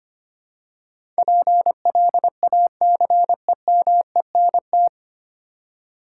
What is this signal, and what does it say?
Morse code, heard as a keyed tone, about 25 wpm, 700 Hz. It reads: PLACEMENT
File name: practice / 25wpm / 9 Letter Words - First 3 Letters Given - 25wpm